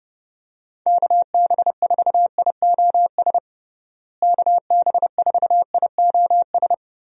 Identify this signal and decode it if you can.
Morse code, heard as a keyed tone, about 30 wpm, 700 Hz. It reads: KB4IOS KB4IOS
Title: USA Call Signs - 6 Characters - Repeated 2x - 30wpm